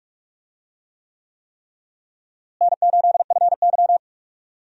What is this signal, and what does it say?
Morse code, heard as a keyed tone, about 41 wpm, 700 Hz. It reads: N8FY
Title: USA Call Signs - 4 Characters - Repeated 3x - 45wpm